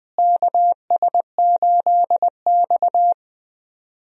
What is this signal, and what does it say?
Morse code, heard as a keyed tone, about 20 wpm, 700 Hz. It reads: KS8X
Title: USA Call Signs - 4 Characters - Repeated 3x - 20wpm